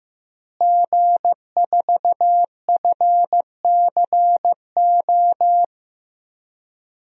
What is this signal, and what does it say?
Morse code, heard as a keyed tone, about 15 wpm, 700 Hz. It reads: G4FCO
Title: DXCC-100 Call Signs - 5 Characters - Repeated 3x - 15wpm